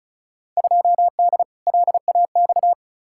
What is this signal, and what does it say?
Morse code, heard as a keyed tone, about 36 wpm, 700 Hz. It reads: 2D LAX
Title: Field Day Exchanges - 35wpm